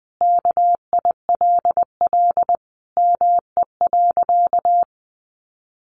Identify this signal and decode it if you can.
Morse code, heard as a keyed tone, about 20 wpm, 700 Hz. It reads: KILL ME.
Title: Sentences from Top 1400 Words 20wpm